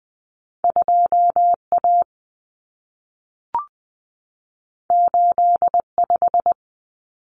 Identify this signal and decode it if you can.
Morse code, heard as a keyed tone, about 20 wpm, 700 Hz. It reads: 2A 85
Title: Common Cut Numbers - 2-Digit 20wpm